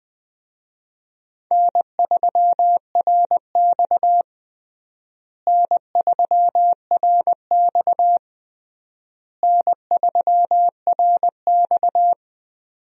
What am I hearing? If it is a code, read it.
Morse code, heard as a keyed tone, about 20 wpm, 700 Hz. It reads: N3RX N3RX N3RX
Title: USA Call Signs - 4 Characters - Repeated 3x - 20wpm